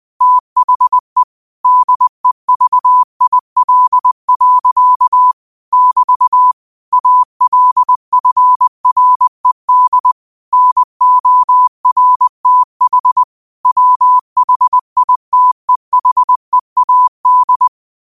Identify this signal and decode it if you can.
Morse code, heard as a keyed tone, about 20 wpm, 1000 Hz. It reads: THE DEVIL. <BT> ALFRED NORTH WHITEHEAD
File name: qotdfiles / 20wpm